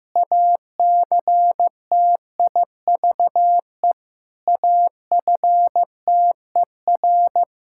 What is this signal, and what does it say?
Morse code, heard as a keyed tone, about 15 wpm, 700 Hz. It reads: ACTIVE AFTER
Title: Sentences from Top 1000 Words - Review Part 6 - 15wpm